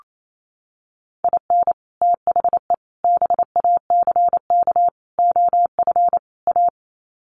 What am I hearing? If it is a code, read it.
Morse code, heard as a keyed tone, about 28 wpm, 700 Hz. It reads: IN THE BACK OF A
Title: Sets of 5 Words Encore2 28wpm